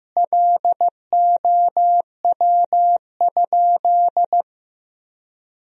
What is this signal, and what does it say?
Morse code, heard as a keyed tone, about 15 wpm, 700 Hz. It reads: LOW?
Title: Sentences from Top 1000 Words - Review Part 4 - 15wpm